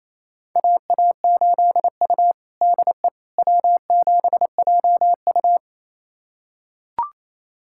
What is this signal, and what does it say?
Morse code, heard as a keyed tone, about 28 wpm, 700 Hz. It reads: AA8U DE W7JU
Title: USA Callsign DE Callsign 28wpm